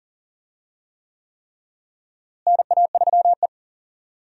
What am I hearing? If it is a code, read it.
Morse code, heard as a keyed tone, about 40 wpm, 700 Hz. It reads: NA3E
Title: USA Call Signs - 4 Characters - Repeated 3x - 40wpm